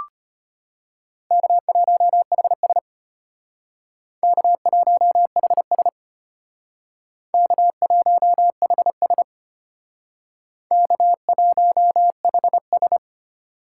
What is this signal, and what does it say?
Morse code, heard as a keyed tone, about 33 wpm, 700 Hz. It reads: K1HS K1HS K1HS K1HS